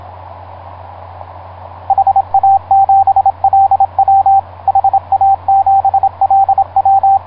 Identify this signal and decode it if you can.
Morse code, heard as a keyed tone, about 27 wpm, 780 Hz, with radio noise. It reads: HA7LW HA7LW
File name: vk9x-g6ay ha7lw etc 10m cw 23feb2010 0836